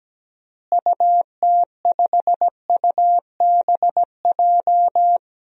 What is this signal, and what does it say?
Morse code, heard as a keyed tone, about 17 wpm, 700 Hz. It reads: UT5UBJ
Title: DXCC-100 Call Signs - 6 Characters - Repeated 3x - 17wpm